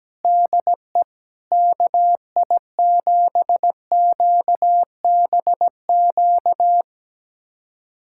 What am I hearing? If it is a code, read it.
Morse code, heard as a keyed tone, about 17 wpm, 700 Hz. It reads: DE KI7QBQ
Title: USA Callsign DE Callsign 17wpm